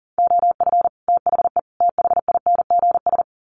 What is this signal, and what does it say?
Morse code, heard as a keyed tone, about 40 wpm, 700 Hz. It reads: OF THE THINGS
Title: Sets of 3 Words 40wpm